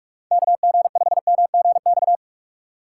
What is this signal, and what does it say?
Morse code, heard as a keyed tone, about 46 wpm, 700 Hz. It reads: KG5MGX
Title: USA Call Signs - 6 Characters - 8x Character Spacing - 45wpm